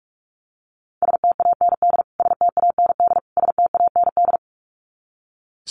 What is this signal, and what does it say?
Morse code, heard as a keyed tone, about 44 wpm, 700 Hz. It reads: STAND STAND STAND
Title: Top 500 Words Repeated 45wpm